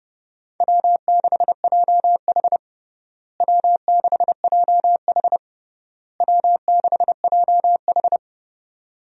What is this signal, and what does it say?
Morse code, heard as a keyed tone, about 32 wpm, 700 Hz. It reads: W6JH W6JH W6JH
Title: USA Call Signs - 4 Characters - Repeated 3x - 30wpm